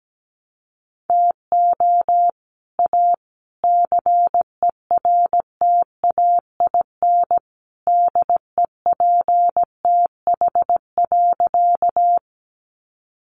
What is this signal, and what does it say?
Morse code, heard as a keyed tone, about 17 wpm, 700 Hz. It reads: TO A CERTAIN DEPTH.
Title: Sentences from Top 1200 Words 17wpm